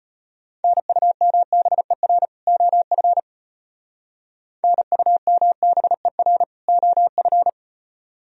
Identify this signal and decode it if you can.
Morse code, heard as a keyed tone, about 32 wpm, 700 Hz. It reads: NUMBER OF NUMBER OF